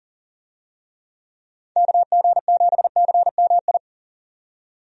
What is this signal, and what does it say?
Morse code, heard as a keyed tone, about 40 wpm, 700 Hz. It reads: KG7CMI
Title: USA Call Signs - 6 Characters - 40wpm